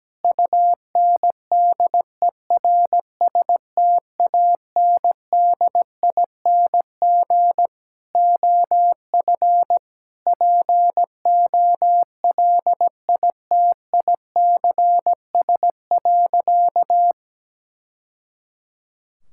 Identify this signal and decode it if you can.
Morse code, heard as a keyed tone, about 17 wpm, 700 Hz. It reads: UNDERSTANDING OF POLITICS.
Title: Sentences from Top 1000 Words - Review Part 6 - 17wpm